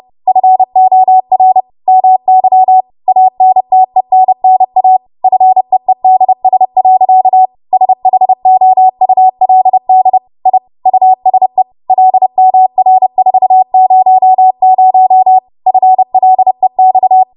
Morse code, heard as a keyed tone, about 30 wpm, 750 Hz. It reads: FOR MY ANTENNA FEEDS. SHOULD I USE LMR400 FLEX
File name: Morse / 30WPM